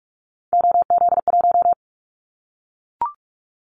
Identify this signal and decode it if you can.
Morse code, heard as a keyed tone, about 44 wpm, 700 Hz. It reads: OZ1
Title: Callsign Prefixes for Top 100 DXCC - 45wpm